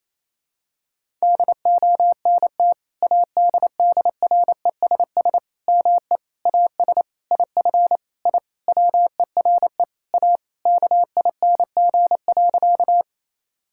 Morse code, heard as a keyed tone, about 27 wpm, 700 Hz. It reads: DONT ADDRESS ME AS IF I WERE A KING.
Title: Sentences from Top 1000 Words - Review Part 6 - 28wpm